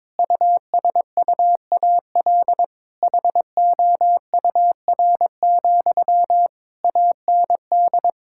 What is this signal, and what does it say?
Morse code, heard as a keyed tone, about 22 wpm, 700 Hz. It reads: USUAL HOUR, AND
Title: Sentences from Top 1600 Words 22wpm